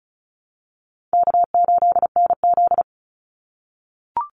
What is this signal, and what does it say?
Morse code, heard as a keyed tone, about 34 wpm, 700 Hz. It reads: K8NZ
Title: Call Signs - 35wpm